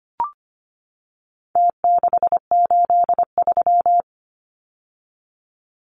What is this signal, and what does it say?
Morse code, heard as a keyed tone, about 25 wpm, 700 Hz. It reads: T683
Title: Numbers - Contest Style - 4-Digit 25wpm